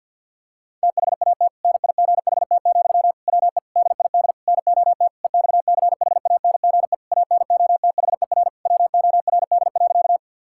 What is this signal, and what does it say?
Morse code, heard as a keyed tone, about 50 wpm, 700 Hz. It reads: THAT NIGHT, WE DID NOT EXCHANGE ANOTHER WORD.
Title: Sentences from Top 1000 Words - Review Part 6 - 50wpm